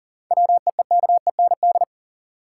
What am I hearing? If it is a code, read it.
Morse code, heard as a keyed tone, about 40 wpm, 700 Hz. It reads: WEEKEND
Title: Seven Letter Words - 40wpm